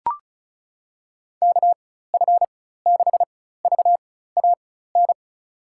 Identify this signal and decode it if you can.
Morse code, heard as a keyed tone, about 36 wpm, spaced as for 16 wpm, 700 Hz. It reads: KF6VAN
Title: USA Call Signs - 6 Characters - 4x Character Spacing - 35wpm